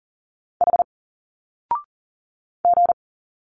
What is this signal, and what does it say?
Morse code, heard as a keyed tone, about 40 wpm, 700 Hz. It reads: H G
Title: Single Letters 40wpm